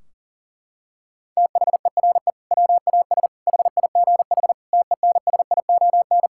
Morse code, heard as a keyed tone, about 40 wpm, 700 Hz. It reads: THERE WAS HIGH TENSION
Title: Sentences from Top 1600 Words 40wpm